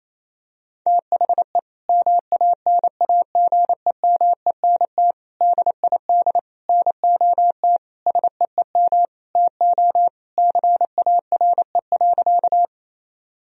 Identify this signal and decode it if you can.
Morse code, heard as a keyed tone, about 28 wpm, 700 Hz. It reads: THE MANAGEMENT DID NOT SEEM TO CARE.
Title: Sentences from Top 1000 Words - Review Part 4 - 28wpm